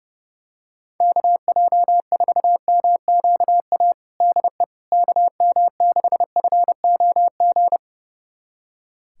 Morse code, heard as a keyed tone, about 32 wpm, 700 Hz. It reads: KJ4MQA DE KM6FOG
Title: USA Callsign DE Callsign - 6 Characters - 30wpm